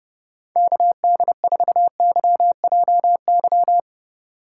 Morse code, heard as a keyed tone, about 29 wpm, 700 Hz. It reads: KD4YJY